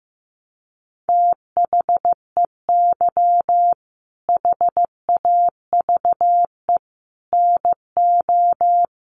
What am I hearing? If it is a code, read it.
Morse code, heard as a keyed tone, about 14 wpm, 700 Hz. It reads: THEY HAVE NO
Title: Sentences from Top 1300 Words 15wpm